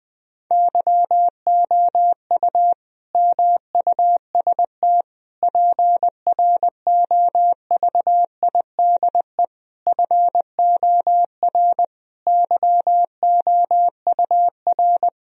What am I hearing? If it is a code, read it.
Morse code, heard as a keyed tone, about 20 wpm, 700 Hz. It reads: YOU MUST PROVIDE FOR YOUR
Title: Sentences from Top 1000 Words - Review Part 3 - 20wpm